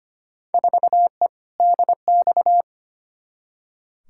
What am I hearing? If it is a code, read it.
Morse code, heard as a keyed tone, about 25 wpm, 700 Hz. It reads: 4E DX
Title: Field Day Exchanges Encore - 25wpm